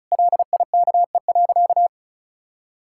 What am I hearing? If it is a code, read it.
Morse code, heard as a keyed tone, about 34 wpm, 700 Hz. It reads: LIKE.
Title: Sentences from Top 1600 Words 35wpm